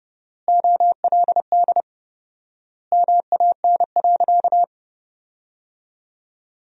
Morse code, heard as a keyed tone, about 32 wpm, 700 Hz. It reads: OLD MAN.
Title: 4x Word-Spacing - Sentences from Top 500 Words - 30wpm